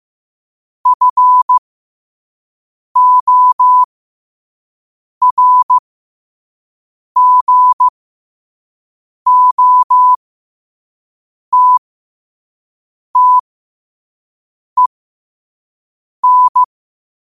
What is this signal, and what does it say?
Morse code, heard as a keyed tone, about 15 wpm, spaced as for 5 wpm, 1000 Hz. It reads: FORGOTTEN
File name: qotdfiles / 5wpm